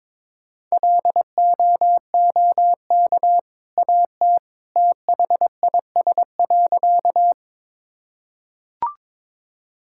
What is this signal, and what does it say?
Morse code, heard as a keyed tone, about 22 wpm, 700 Hz. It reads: LOOK AT THIS.
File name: practice / 22wpm / Sentences with 2-Word Phrases - 22wpm